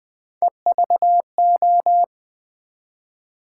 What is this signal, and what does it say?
Morse code, heard as a keyed tone, about 20 wpm, 700 Hz. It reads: EVO
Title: Build A Word - 1 Letter At A Time - 3 to 5-Letter Words - 20wpm